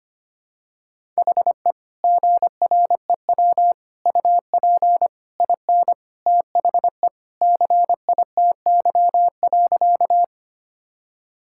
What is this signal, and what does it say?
Morse code, heard as a keyed tone, about 25 wpm, 700 Hz. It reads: HE GREW UP IN THE CITY.
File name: practice / 25wpm / Sentences with 3-Word Phrases - 25wpm